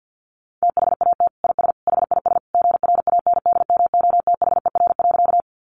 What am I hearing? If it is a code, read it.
Morse code, heard as a keyed tone, about 50 wpm, 700 Hz. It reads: THAT IS HIS GRANDMOTHER.
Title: Sentences from Top 1200 Words 50wpm